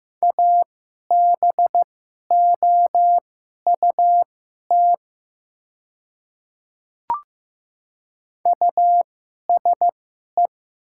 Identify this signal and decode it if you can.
Morse code, heard as a keyed tone, about 15 wpm, spaced as for 11 wpm, 700 Hz. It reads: ABOUT USE
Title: 2x Character Spacing - Top 100 Words - 15wpm